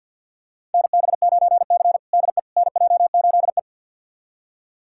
Morse code, heard as a keyed tone, about 52 wpm, 700 Hz. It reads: NB9C DE NJ8E
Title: USA Callsign DE Callsign - 4 Characters - 50wpm